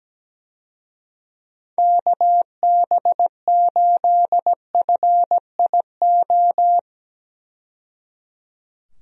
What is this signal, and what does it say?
Morse code, heard as a keyed tone, about 17 wpm, 700 Hz. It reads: KB8FIO
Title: USA Call Signs - 6 Characters - 17wpm